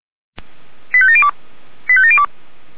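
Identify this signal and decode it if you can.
Five-tone selective call (ZVEI), heard as quick runs of five tones, tones of about 70 ms. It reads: 85702 85702
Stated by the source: ZVEI five-tone call received by SDR, from audio.f85360.feuerwehr-orlamuende.org (Feuerwehr Orlamünde fire brigade server).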